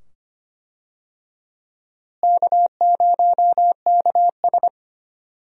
Morse code, heard as a keyed tone, about 25 wpm, 700 Hz. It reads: K0KS
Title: USA Call Signs - 4 Characters - Repeated 2x - 25wpm